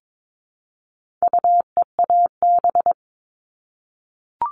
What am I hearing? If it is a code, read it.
Morse code, heard as a keyed tone, about 22 wpm, 700 Hz. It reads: UEAB
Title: All Cut Numbers - 4-Digit 22wpm